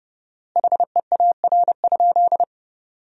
Morse code, heard as a keyed tone, about 30 wpm, 700 Hz. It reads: HEAR?
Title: 4x Word-Spacing - Sentences from Top 300 Words - 30wpm